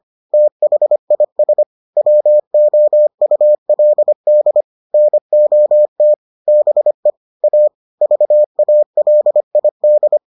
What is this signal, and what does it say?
Morse code, heard as a keyed tone, about 25 wpm, 600 Hz. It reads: THIS WOULD NOT BE A VALID